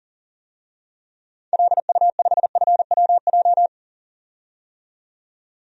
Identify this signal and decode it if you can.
Morse code, heard as a keyed tone, about 38 wpm, 700 Hz. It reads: LU5FWJ
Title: DXCC-100 Call Signs - 6 Characters - 40wpm